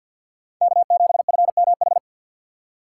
Morse code, heard as a keyed tone, about 53 wpm, 700 Hz. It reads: K7FMH
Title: USA Call Signs - 5 Characters - 4x Character Spacing - 50wpm